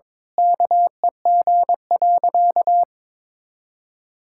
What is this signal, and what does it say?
Morse code, heard as a keyed tone, about 22 wpm, 700 Hz. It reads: KEG.
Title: Pangram Challenge v2 - 22wpm